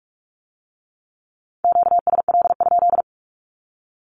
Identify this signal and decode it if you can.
Morse code, heard as a keyed tone, about 46 wpm, 700 Hz. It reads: QSL?
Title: Q-Codes with Spoken Meaning - 45wpm